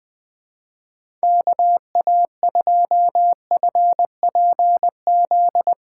Morse code, heard as a keyed tone, about 21 wpm, 700 Hz. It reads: KA2FPZ
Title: USA Callsign DE Callsign - 6 Characters - 20wpm